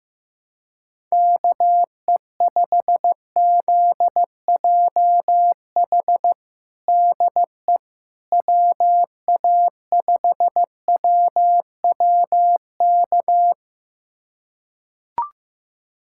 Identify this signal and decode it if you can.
Morse code, heard as a keyed tone, about 16 wpm, 700 Hz. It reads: KE5ZJH DE WA5WWK
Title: USA Callsign DE Callsign - 6 Characters - 15wpm